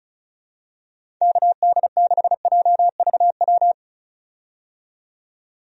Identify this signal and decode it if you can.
Morse code, heard as a keyed tone, about 36 wpm, 700 Hz. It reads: KD6JVW
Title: USA Call Signs - 6 Characters - 35wpm